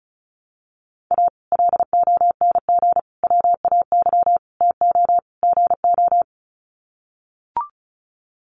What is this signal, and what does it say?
Morse code, heard as a keyed tone, about 34 wpm, 700 Hz. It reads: A LONG WAY TO GO
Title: Sets of 5 Words 35wpm